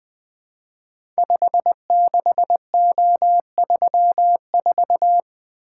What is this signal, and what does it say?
Morse code, heard as a keyed tone, about 20 wpm, 700 Hz. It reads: 56O34
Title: Code Groups - 5 Letter:Numbers - 20wpm